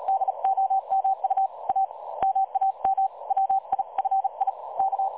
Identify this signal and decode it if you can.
Morse code, heard as a keyed tone, about 36 wpm, 770 Hz. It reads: DOMU A MAM WI FI K
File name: dlazaawansowanych / nagraniazpasma / websdr recording 3565.0kHz